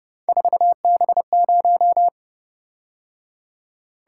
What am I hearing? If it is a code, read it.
Morse code, heard as a keyed tone, about 30 wpm, 700 Hz. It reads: 4B0
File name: Speed-Racing - Three Letter-Number Combinations - 25wpm